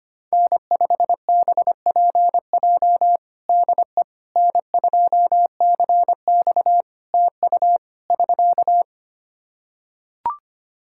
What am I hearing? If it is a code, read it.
Morse code, heard as a keyed tone, about 25 wpm, 700 Hz. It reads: N5BPJ DE N2CX TU <SK>
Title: K1USN SST QSO Practice 25wpm